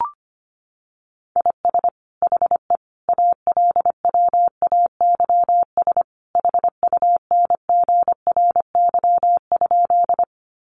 Morse code, heard as a keyed tone, about 25 wpm, 700 Hz. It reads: IS HE ALWAYS HUNGRY?
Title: Sentences from Top 1400 Words 25wpm